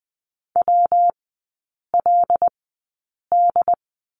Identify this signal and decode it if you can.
Morse code, heard as a keyed tone, about 20 wpm, spaced as for 8 wpm, 700 Hz. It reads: WLD
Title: 191 Lesson 31e - Letter v - Three Characters 20wpm